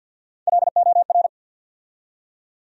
Morse code, heard as a keyed tone, about 53 wpm, 700 Hz. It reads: LOR
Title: USA Call Sign Postfixes - 3 Letters 50wpm